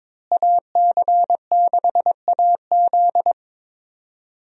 Morse code, heard as a keyed tone, about 22 wpm, 700 Hz. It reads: AC6AZ
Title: USA Call Signs - 5 Characters - Repeated 3x - 22wpm